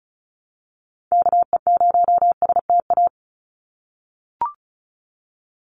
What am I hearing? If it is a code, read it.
Morse code, heard as a keyed tone, about 34 wpm, 700 Hz. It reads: KE0STA
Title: USA Call Signs 35wpm